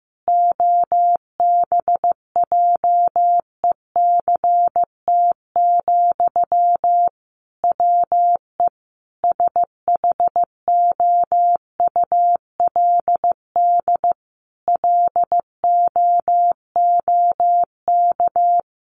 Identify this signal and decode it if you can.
Morse code, heard as a keyed tone, about 15 wpm, 700 Hz. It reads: OBJECT, WE SHOULD LOOK
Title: Sentences from Top 800 Words 15wpm